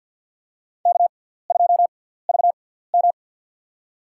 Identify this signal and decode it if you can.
Morse code, heard as a keyed tone, about 49 wpm, spaced as for 17 wpm, 700 Hz. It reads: K2VM
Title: USA Call Signs - 6x Character Spacing - 50wpm